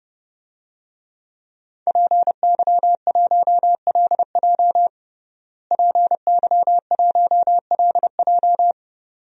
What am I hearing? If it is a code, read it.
Morse code, heard as a keyed tone, about 30 wpm, 700 Hz. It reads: PY1LJ PY1LJ
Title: DXCC-100 Call Signs - 5 Characters - Repeated 2x - 30wpm